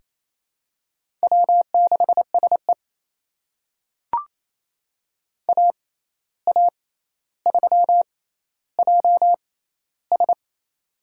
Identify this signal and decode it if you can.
Morse code, heard as a keyed tone, about 28 wpm, spaced as for 13 wpm, 700 Hz. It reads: W6SE AA3JS
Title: USA Call Signs - 6x Character Spacing - 28wpm